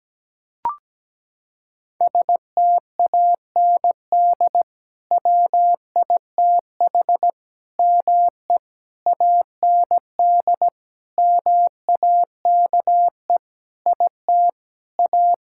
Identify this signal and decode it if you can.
Morse code, heard as a keyed tone, about 17 wpm, 700 Hz. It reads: STAND WITH ME AND MAKE IT A
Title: Sentences from Top 1600 Words 17wpm